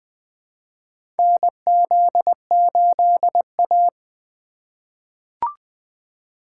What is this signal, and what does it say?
Morse code, heard as a keyed tone, about 20 wpm, 700 Hz. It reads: NZ8A
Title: USA Call Signs - 4 Characters - 3x Character Spacing - 20wpm